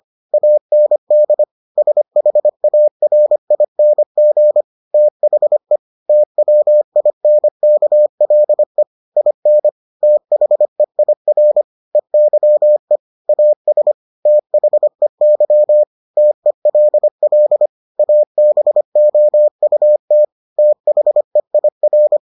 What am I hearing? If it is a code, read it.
Morse code, heard as a keyed tone, about 25 wpm, 600 Hz. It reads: AND SHARING THE TWINKLE IN THEIR EYE AS THEY TELL ABOUT THEIR